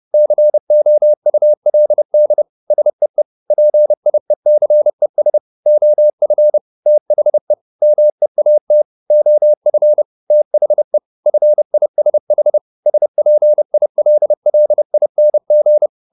Morse code, heard as a keyed tone, about 30 wpm, 600 Hz. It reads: COULD SEE PIECES OF THE MEAT OF THE FISH SPILLING